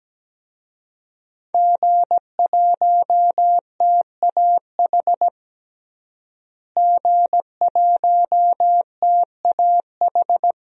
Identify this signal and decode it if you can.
Morse code, heard as a keyed tone, about 17 wpm, 700 Hz. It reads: G1TAH G1TAH
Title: DXCC-100 Call Signs - 5 Characters - Repeated 3x - 17wpm